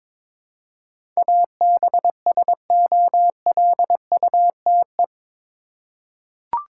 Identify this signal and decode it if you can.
Morse code, heard as a keyed tone, about 22 wpm, 700 Hz. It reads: ABSOLUTE
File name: practice / 22wpm / Eight Letter Words - 22wpm